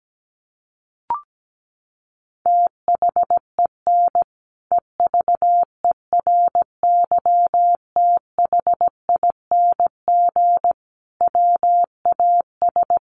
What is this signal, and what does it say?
Morse code, heard as a keyed tone, about 17 wpm, 700 Hz. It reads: THEN EVERYTHING WAS
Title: Sentences from Top 700 Words 17wpm